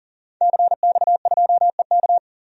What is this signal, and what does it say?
Morse code, heard as a keyed tone, about 37 wpm, 700 Hz. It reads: CX2EK
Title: DXCC-100 Call Signs - 5 Characters - 40wpm